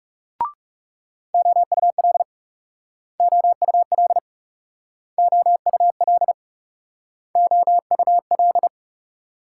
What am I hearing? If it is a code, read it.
Morse code, heard as a keyed tone, about 40 wpm, 700 Hz. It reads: OUL OUL OUL OUL